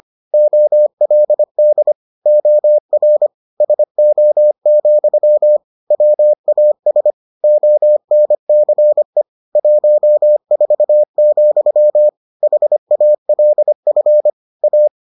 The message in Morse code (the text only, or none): OLD OR SO, WAS ONCE 14, HALF A